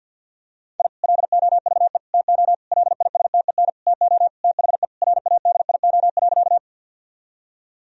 Morse code, heard as a keyed tone, about 51 wpm, 700 Hz. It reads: I LOVE TO LISTEN TO THE RADIO.
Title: Sentences from Top 1000 Words - Review Part 3 - 50wpm